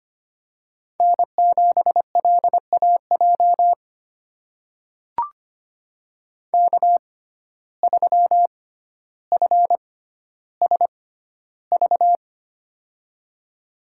Morse code, heard as a keyed tone, about 25 wpm, spaced as for 13 wpm, 700 Hz. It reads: N7LAJ K3FSV
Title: USA Call Signs - 5 Characters - 6x Character Spacing - 25wpm